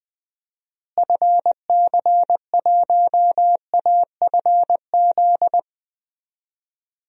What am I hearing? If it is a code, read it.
Morse code, heard as a keyed tone, about 20 wpm, 700 Hz. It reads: FC1AFZ
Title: DXCC-100 Call Signs - 6 Characters - Repeated 2x - 20wpm